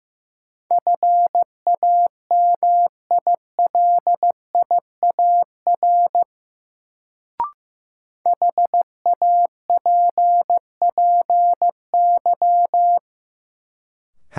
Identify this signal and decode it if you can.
Morse code, heard as a keyed tone, about 15 wpm, 700 Hz. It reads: FAMILIAR HAPPY
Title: Sentences from Top 1000 Words - New Word Drill - 15wpm